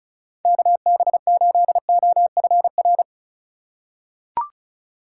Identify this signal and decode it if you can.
Morse code, heard as a keyed tone, about 34 wpm, 700 Hz. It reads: KB8OFR
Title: USA Call Signs - 6 Characters - 3x Character Spacing - 35wpm